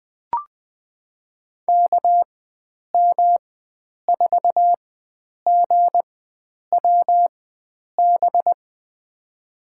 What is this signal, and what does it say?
Morse code, heard as a keyed tone, about 20 wpm, spaced as for 9 wpm, 700 Hz. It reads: KM4GWB
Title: USA Call Signs - 6 Characters - 4x Character Spacing - 20wpm